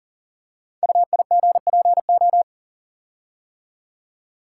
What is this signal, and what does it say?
Morse code, heard as a keyed tone, about 39 wpm, 700 Hz. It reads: UIGPO
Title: Code Groups - 5 Letters - 40wpm